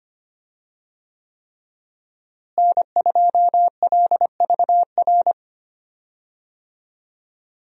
Morse code, heard as a keyed tone, about 25 wpm, 700 Hz. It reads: N2LVR
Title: USA Call Signs - 5 Characters - Repeated 3x - 25wpm